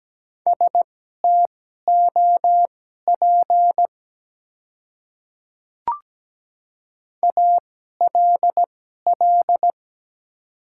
Morse code, heard as a keyed tone, about 17 wpm, spaced as for 12 wpm, 690 Hz. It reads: STOP ALL
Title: 2x Character Spacing - Top 300 Words - 17wpm